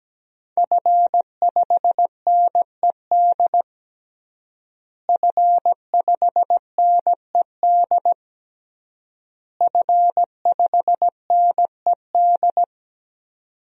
Morse code, heard as a keyed tone, about 17 wpm, 700 Hz. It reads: F5NED F5NED F5NED
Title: DXCC-100 Call Signs - 5 Characters - Repeated 3x - 17wpm